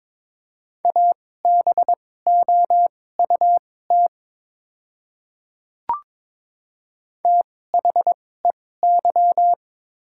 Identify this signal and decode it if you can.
Morse code, heard as a keyed tone, about 22 wpm, spaced as for 16 wpm, 700 Hz. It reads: ABOUT THEY
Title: 2x Character Spacing - Top 300 Words - 22wpm